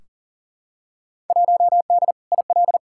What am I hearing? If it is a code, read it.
Morse code, heard as a keyed tone, about 40 wpm, 700 Hz. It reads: 1D IL
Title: Field Day Exchanges - 40wpm